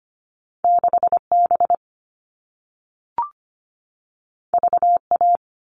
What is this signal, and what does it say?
Morse code, heard as a keyed tone, about 25 wpm, 700 Hz. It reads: 6B VA
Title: All Cut Numbers - 2-Digit 25wpm